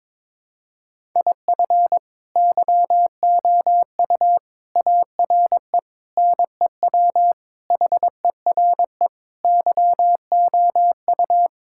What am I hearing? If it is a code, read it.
Morse code, heard as a keyed tone, about 21 wpm, 700 Hz. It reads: IF YOU ARE NEW HERE YOU
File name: practice / 22wpm / Sentences with 8-Word Phrases - 22wpm